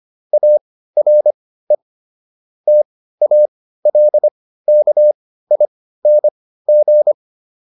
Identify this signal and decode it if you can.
Morse code, heard as a keyed tone, about 25 wpm, spaced as for 15 wpm, 600 Hz. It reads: ARE TALKING